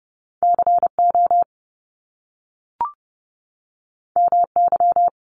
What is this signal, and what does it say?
Morse code, heard as a keyed tone, about 30 wpm, 700 Hz. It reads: CO MY
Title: Top Two Letter Combinations in Words 30wpm